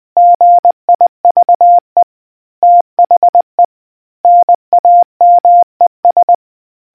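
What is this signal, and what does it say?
Morse code, heard as a keyed tone, about 20 wpm, 700 Hz. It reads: GIVE THE NAMES
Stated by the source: Morse quiz question for naval history buffs from the RNZN Communicators Association.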